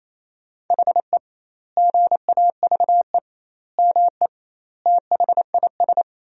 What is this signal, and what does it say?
Morse code, heard as a keyed tone, about 28 wpm, 700 Hz. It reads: HE GAVE ME THIS
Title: 2x Word-Spacing - Sentences from Top 500 Words - 28wpm